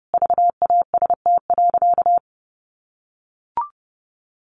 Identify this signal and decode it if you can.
Morse code, heard as a keyed tone, about 30 wpm, 700 Hz. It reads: VAST.
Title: Sentences from Top 1300 Words 30wpm